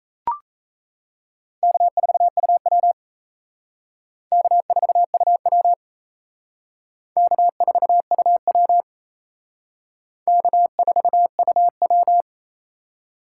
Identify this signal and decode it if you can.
Morse code, heard as a keyed tone, about 37 wpm, 700 Hz. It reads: K4UW K4UW K4UW K4UW